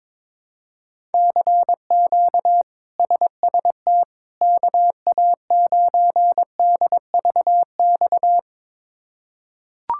Morse code, heard as a keyed tone, about 21 wpm, 700 Hz. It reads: CQ SST KA9DVX
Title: K1USN SST QSO Practice - 22wpm